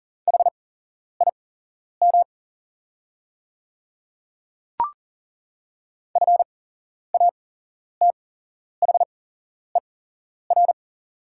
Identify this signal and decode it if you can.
Morse code, heard as a keyed tone, about 38 wpm, spaced as for 11 wpm, 700 Hz. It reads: HIM FATHER
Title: 8x Character Spacing - Top 200 Words - 40wpm